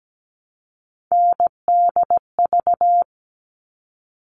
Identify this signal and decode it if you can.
Morse code, heard as a keyed tone, about 17 wpm, 700 Hz. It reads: NDV
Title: All Cut Numbers - 3-Digit 17wpm